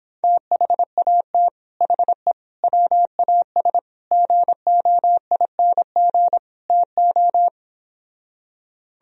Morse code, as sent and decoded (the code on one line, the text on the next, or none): - .... .- - / .... / .-- .- ... / --. --- .. -. --. / - ---
THAT HE WAS GOING TO